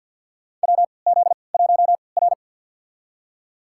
Morse code, heard as a keyed tone, about 53 wpm, spaced as for 29 wpm, 700 Hz. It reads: WZ1R
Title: USA Call Signs - 3x Character Spacing - 50wpm